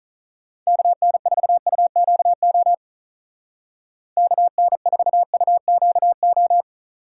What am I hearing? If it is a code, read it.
Morse code, heard as a keyed tone, about 44 wpm, 700 Hz. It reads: KN4UQO KN4UQO